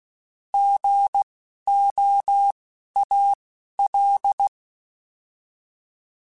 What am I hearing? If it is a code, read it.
Morse code, heard as a keyed tone, about 16 wpm, spaced as for 11 wpm, 800 Hz. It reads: GOAL
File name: Alphabetical / Alphabetical mp3s / Gs @10